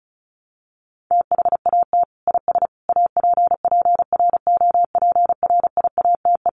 Morse code, heard as a keyed tone, about 34 wpm, 700 Hz. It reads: THAT IS APPROPRIATE
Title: Sentences from Top 900 Words 35wpm